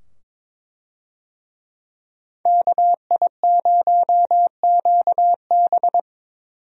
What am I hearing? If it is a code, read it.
Morse code, heard as a keyed tone, about 22 wpm, 700 Hz. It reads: KI0QB
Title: USA Call Signs - 5 Characters - 22wpm